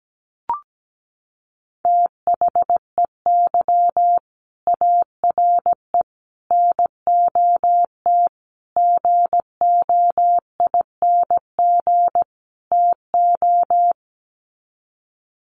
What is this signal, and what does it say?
Morse code, heard as a keyed tone, about 17 wpm, 700 Hz. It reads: THEY ARE NOT GOING TO
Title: Sets of 5 Words Encore2 17wpm